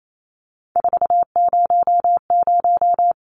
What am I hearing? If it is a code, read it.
Morse code, heard as a keyed tone, about 28 wpm, 700 Hz. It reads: 400
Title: Numbers - 3-Digit 28wpm